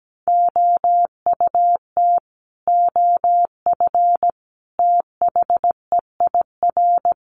Morse code, heard as a keyed tone, about 17 wpm, 700 Hz. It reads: OUT OF THEIR
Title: Sets of 5 Words Encore2 17wpm